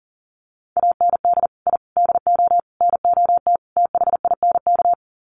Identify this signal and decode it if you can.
Morse code, heard as a keyed tone, about 40 wpm, 700 Hz. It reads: AND I DO NOT THINK
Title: Sets of 5 Words 40wpm